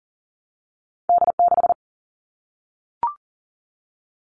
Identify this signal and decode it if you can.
Morse code, heard as a keyed tone, about 40 wpm, 700 Hz. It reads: D6
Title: All Cut Numbers - 2-Digit 40wpm